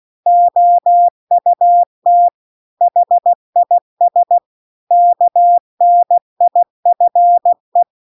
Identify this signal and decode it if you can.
Morse code, heard as a keyed tone, about 16 wpm, 700 Hz. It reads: OUT HIS KNIFE
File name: CW-The-Big-Twoheaded-River-2-16WPM